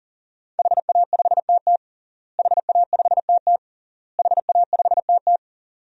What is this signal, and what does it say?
Morse code, heard as a keyed tone, about 40 wpm, 700 Hz. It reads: HA5TT HA5TT HA5TT
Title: DXCC-100 Call Signs - 5 Characters - Repeated 3x - 40wpm